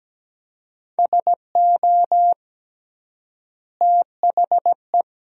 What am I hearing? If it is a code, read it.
Morse code, heard as a keyed tone, about 17 wpm, 700 Hz. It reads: SO THE
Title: 3x Word-Spacing - Sentences from Top 400 Words - 17wpm